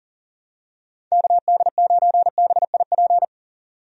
Morse code, heard as a keyed tone, about 40 wpm, 700 Hz. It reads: KD9BIP
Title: USA Call Signs - 6 Characters - 40wpm